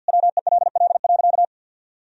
Morse code, heard as a keyed tone, about 51 wpm, 700 Hz. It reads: WELL.
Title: Sentences from Top 1000 Words - Review Part 3 - 50wpm